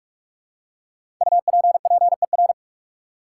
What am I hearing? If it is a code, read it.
Morse code, heard as a keyed tone, about 46 wpm, 700 Hz. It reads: UPPER
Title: Five Letter Words - 45wpm